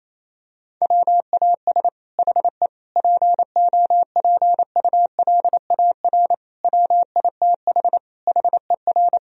Morse code, heard as a keyed tone, about 28 wpm, 700 Hz. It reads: WAS HE POPULAR WITH HER
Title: Sentences from Top 1000 Words - Review Part 3 - 28wpm